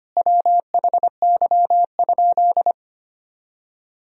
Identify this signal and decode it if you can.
Morse code, heard as a keyed tone, about 25 wpm, 700 Hz. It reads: WHY?
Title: Sentences from Top 200 Words and Abbreviations 25wpm